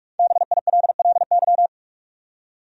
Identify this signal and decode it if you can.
Morse code, heard as a keyed tone, about 46 wpm, 700 Hz. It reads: BILLY
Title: USA Common Names - Max 5 Letters - 45wpm